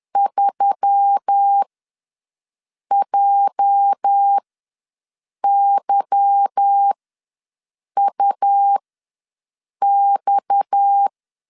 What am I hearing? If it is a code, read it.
Morse code, heard as a keyed tone, about 10 wpm, spaced as for 6 wpm, 800 Hz. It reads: 3JYUX
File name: LEZIONI / 10 SIGNS / 10A SLASH ws 10wpm 4121